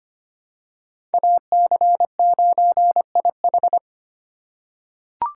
Morse code, heard as a keyed tone, about 25 wpm, 700 Hz. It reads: AC9IH
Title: USA Call Signs - 5 Characters - 2x Character Spacing - 25wpm